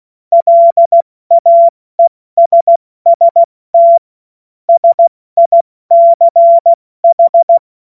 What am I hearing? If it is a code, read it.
Morse code, heard as a keyed tone, about 16 wpm, 680 Hz. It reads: LAESST SICH
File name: Morsekurs / Witze / Witz 06. September 2022 70 F80